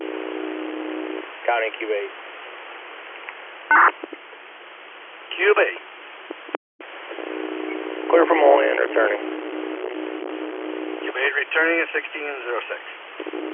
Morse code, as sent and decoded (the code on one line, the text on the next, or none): none